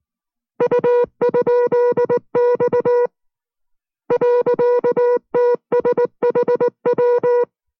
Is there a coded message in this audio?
Yes: Morse code from 0.6 to 7.4 s, heard as a keyed tone, about 19 wpm, 490 Hz. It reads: U?X .TSHW